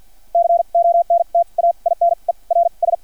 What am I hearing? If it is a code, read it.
Morse code, heard as a keyed tone, about 50 wpm, spaced as for 37 wpm, 680 Hz. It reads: CONTAINERS
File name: archived / CRSnet